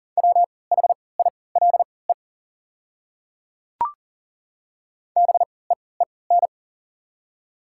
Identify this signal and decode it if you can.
Morse code, heard as a keyed tone, about 40 wpm, spaced as for 23 wpm, 690 Hz. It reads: WHILE BEEN
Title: 3x Character Spacing - Top 500 Words - 40wpm